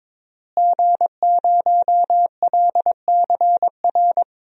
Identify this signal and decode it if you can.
Morse code, heard as a keyed tone, about 21 wpm, 700 Hz. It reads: G0LCR
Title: DXCC-100 Call Signs - 5 Characters - Repeated 2x - 22wpm